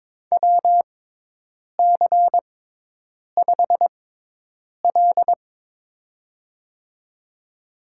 Morse code, heard as a keyed tone, about 22 wpm, spaced as for 8 wpm, 700 Hz. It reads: WC5L
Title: USA Call Signs - 4 Characters - 6x Character Spacing - 22wpm